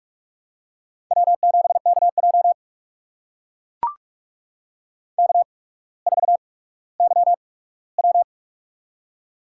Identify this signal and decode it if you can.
Morse code, heard as a keyed tone, about 44 wpm, spaced as for 19 wpm, 700 Hz. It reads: W7KJ K4YW
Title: USA Call Signs - 4 Characters - 8x Character Spacing - 45wpm